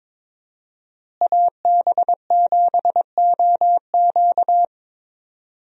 Morse code, heard as a keyed tone, about 22 wpm, 700 Hz. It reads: AB7OQ
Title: USA Call Signs - 5 Characters - Repeated 3x - 22wpm